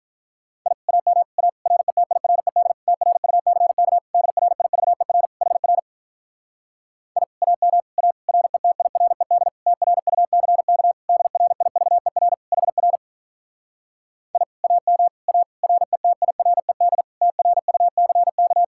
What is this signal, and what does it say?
Morse code, heard as a keyed tone, about 48 wpm, 700 Hz. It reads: I AM A RETIRED TRUCK DRIVER HR I AM A RETIRED TRUCK DRIVER HR I AM A RETIRED TRUCK